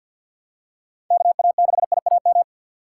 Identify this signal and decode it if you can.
Morse code, heard as a keyed tone, about 50 wpm, 700 Hz. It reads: KA6IAM
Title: USA Call Signs - 6 Characters - 3x Character Spacing - 50wpm